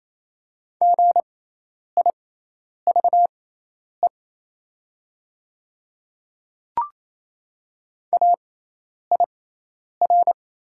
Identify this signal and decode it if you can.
Morse code, heard as a keyed tone, about 28 wpm, spaced as for 10 wpm, 700 Hz. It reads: GIVE AIR
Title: 6x Character Spacing - Top 200 Words - 28wpm